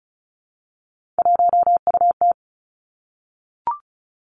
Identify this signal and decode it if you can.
Morse code, heard as a keyed tone, about 36 wpm, 700 Hz. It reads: 1UT
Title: Three Challenging Letter-Number Combinations 35wpm